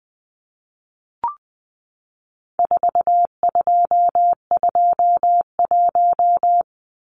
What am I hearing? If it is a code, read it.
Morse code, heard as a keyed tone, about 20 wpm, 700 Hz. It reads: 4221
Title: Numbers - 4-Digit 20wpm